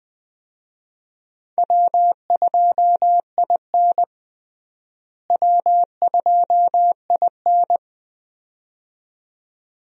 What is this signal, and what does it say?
Morse code, heard as a keyed tone, about 20 wpm, 700 Hz. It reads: W2IN W2IN
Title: USA Call Signs - 4 Characters - Repeated 2x - 20wpm